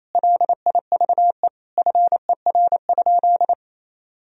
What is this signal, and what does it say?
Morse code, heard as a keyed tone, about 28 wpm, 700 Hz. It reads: LIVE FER?
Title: Sentences from Top 200 Words and Abbreviations 28wpm